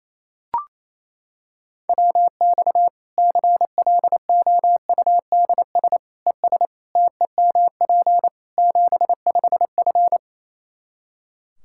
Morse code, heard as a keyed tone, about 28 wpm, 700 Hz. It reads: WX CLOUDS ES TEMP 75F
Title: K1USN SST QSO Practice 28wpm